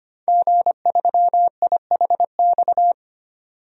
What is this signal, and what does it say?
Morse code, heard as a keyed tone, about 25 wpm, 700 Hz. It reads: G3IHX
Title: DXCC-100 Call Signs - 5 Characters - 25wpm